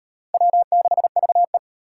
Morse code, heard as a keyed tone, about 37 wpm, 700 Hz. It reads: W6VE